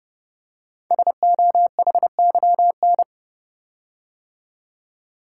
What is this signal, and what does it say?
Morse code, heard as a keyed tone, about 30 wpm, 700 Hz. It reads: SOHYN
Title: Code Groups - 5 Letters - 30wpm